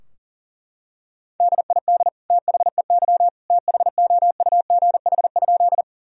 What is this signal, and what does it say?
Morse code, heard as a keyed tone, about 37 wpm, 700 Hz. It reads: DID THEY THOUGH?
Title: Sentences from Top 1000 Words - Review Part 2 - 40wpm